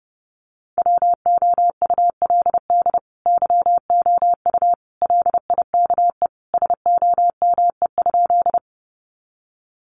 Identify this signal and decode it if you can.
Morse code, heard as a keyed tone, about 30 wpm, 700 Hz. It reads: WOULD YOU LIKE SOME?
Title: Sentences from Top 100 Words 30wpm